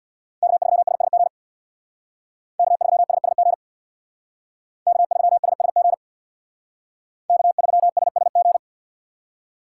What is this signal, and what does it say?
Morse code, heard as a keyed tone, about 62 wpm, 700 Hz. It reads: K3SSG K3SSG K3SSG K3SSG
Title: Speed Racing - USA 1x3 Call Signs 50wpm